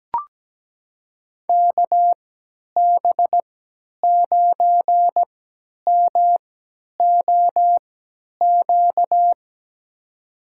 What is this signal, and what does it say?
Morse code, heard as a keyed tone, about 17 wpm, spaced as for 10 wpm, 700 Hz. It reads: KB9MOQ
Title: USA Call Signs - 6 Characters - 3x Character Spacing - 17wpm